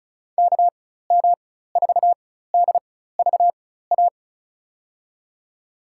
Morse code, heard as a keyed tone, about 34 wpm, spaced as for 16 wpm, 690 Hz. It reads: KM4DVA
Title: USA Call Signs - 4x Character Spacing - 35wpm